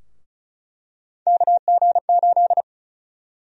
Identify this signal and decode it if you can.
Morse code, heard as a keyed tone, about 36 wpm, 700 Hz. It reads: KG8
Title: USA Call Sign Prefixes - Repeated 2x - 35wpm